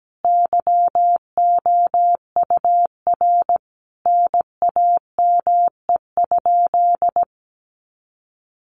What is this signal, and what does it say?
Morse code, heard as a keyed tone, about 17 wpm, 700 Hz. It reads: YOUR NAME?
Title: Sentences from Top 1200 Words 17wpm